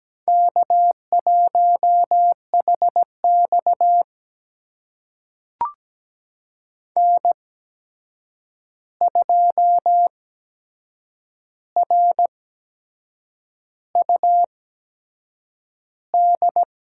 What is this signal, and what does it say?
Morse code, heard as a keyed tone, about 17 wpm, spaced as for 7 wpm, 700 Hz. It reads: K1HX N2RUD
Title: USA Call Signs - 8x Character Spacing - 17wpm